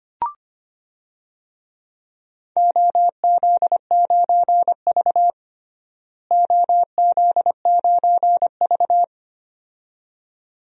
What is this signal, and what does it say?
Morse code, heard as a keyed tone, about 25 wpm, 700 Hz. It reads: OZ9V OZ9V
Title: DXCC-100 Call Signs - 4 Characters - Repeated 2x - 25wpm